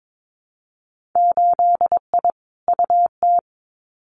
Morse code, heard as a keyed tone, about 22 wpm, 700 Hz. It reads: 8I UT
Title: Winter Field Day Exchanges Encore 22wpm